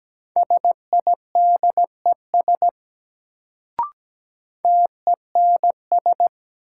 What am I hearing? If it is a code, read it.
Morse code, heard as a keyed tone, about 17 wpm, 700 Hz. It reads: SIDES TENS
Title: Sets of 4 Words - New Word Drill - 17wpm